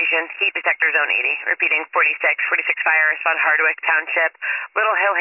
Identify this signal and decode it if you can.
Emergency Alert System (SAME) tones, heard as harsh buzzing bursts, with speech. It is heard from 0.6 to 4.3 s.